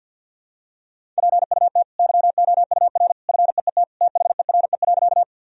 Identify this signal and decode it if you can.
Morse code, heard as a keyed tone, about 50 wpm, 700 Hz. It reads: PUT YOUR FEET THERE.
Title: Sentences from Top 1000 Words - Review Part 1 - 50wpm